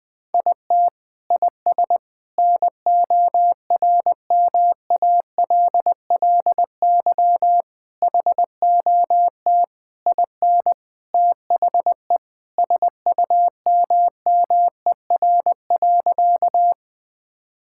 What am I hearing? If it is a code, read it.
Morse code, heard as a keyed tone, about 20 wpm, 700 Hz. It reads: IT IS NORMALLY HOT IN THE SUMMER.
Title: Sentences from Top 1000 Words - Review Part 6 - 20wpm